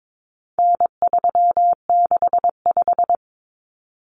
Morse code, heard as a keyed tone, about 22 wpm, 700 Hz. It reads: N365
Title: Common Cut Numbers - 4-Digit 22wpm